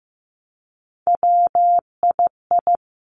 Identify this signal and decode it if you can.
Morse code, heard as a keyed tone, about 15 wpm, 700 Hz. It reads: WII
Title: Three Challenging Letter-Number Combinations 15wpm